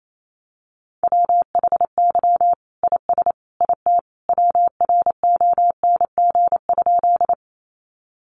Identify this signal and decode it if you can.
Morse code, heard as a keyed tone, about 28 wpm, 700 Hz. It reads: WHY IS IT WRONG?
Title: Sentences from Top 900 Words 28wpm